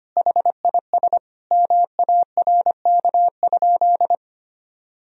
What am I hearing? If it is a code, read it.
Morse code, heard as a keyed tone, about 25 wpm, 700 Hz. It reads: HIS MARK?
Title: Sentences from Top 300 Words and Abbreviations 25wpm